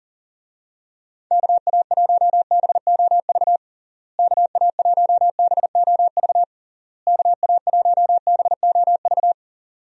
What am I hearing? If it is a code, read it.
Morse code, heard as a keyed tone, about 40 wpm, 700 Hz. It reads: KA1BOV KA1BOV KA1BOV
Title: USA Call Signs - 6 Characters - Repeated 3x - 40wpm